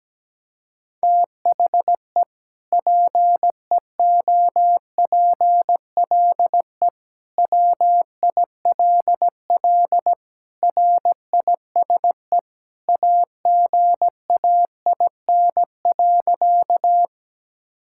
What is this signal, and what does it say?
Morse code, heard as a keyed tone, about 17 wpm, 700 Hz. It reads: THE PEOPLE WILL RISE AGAIN.
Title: Sentences from Top 1000 Words - Review Part 6 - 17wpm